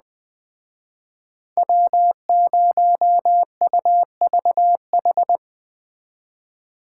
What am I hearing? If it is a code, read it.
Morse code, heard as a keyed tone, about 20 wpm, 700 Hz. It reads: W0UVH